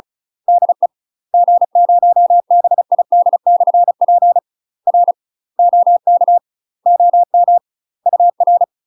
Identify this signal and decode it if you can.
Morse code, heard as a keyed tone, about 34 wpm, 700 Hz. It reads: DE G0BID/P R OK OM UR
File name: UK MASTER QSOs TRAINING / 35WPM / UK MASTER QSOs 0008